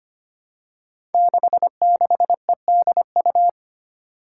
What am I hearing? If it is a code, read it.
Morse code, heard as a keyed tone, about 25 wpm, 700 Hz. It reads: 66EDU
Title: Five Challenging Letter-Number Combinations 25wpm